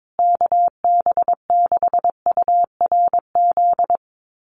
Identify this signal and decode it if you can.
Morse code, heard as a keyed tone, about 22 wpm, 700 Hz. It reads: KB6URZ
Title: USA Call Signs Repeated 3x 22wpm